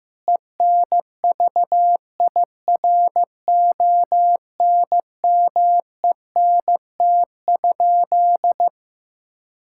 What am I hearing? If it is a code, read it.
Morse code, heard as a keyed tone, about 16 wpm, 700 Hz. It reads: ENVIRONMENT?
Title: Sentences from Top 1000 Words - Review Part 5 - 15wpm